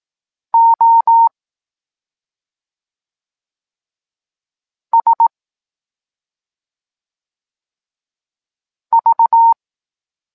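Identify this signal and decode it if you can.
Morse code, heard as a keyed tone, about 18 wpm, 920 Hz. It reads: O S V
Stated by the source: Evenly spaced Morse lesson text 21 for learning letters G and Z, from SP9MOA Niepołomicki Klub Krótkofalowców.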